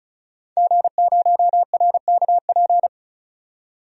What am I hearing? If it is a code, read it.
Morse code, heard as a keyed tone, about 36 wpm, 700 Hz. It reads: G0RKP
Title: DXCC-100 Call Signs - 5 Characters - 35wpm